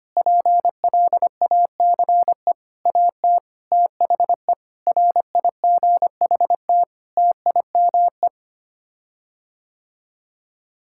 Morse code, heard as a keyed tone, about 25 wpm, 700 Hz. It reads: PLACE AT THE RIGHT TIME
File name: Sets of 9 Words - 25wpm